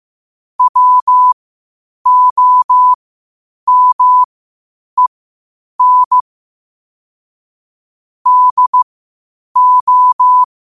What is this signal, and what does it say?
Morse code, heard as a keyed tone, about 15 wpm, spaced as for 8 wpm, 1000 Hz. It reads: WOMEN DO
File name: qotdfiles / 8wpm